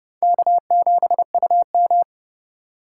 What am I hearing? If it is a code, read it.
Morse code, heard as a keyed tone, about 40 wpm, spaced as for 35 wpm, 700 Hz. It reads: K7UM
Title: Speed Racing - USA 1x2 Call Signs 30wpm